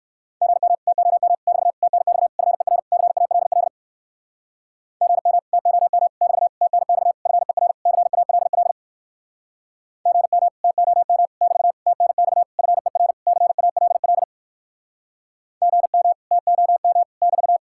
Morse code, heard as a keyed tone, about 59 wpm, 700 Hz. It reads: GM TOM <BT> TNX FER CALL GM TOM <BT> TNX FER CALL GM TOM <BT> TNX FER CALL GM TOM <BT>